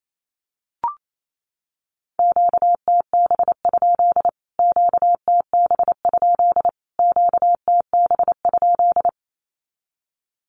Morse code, heard as a keyed tone, about 28 wpm, 700 Hz. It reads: QTB? QTB? QTB?
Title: Q-Codes Repeated 3x 28wpm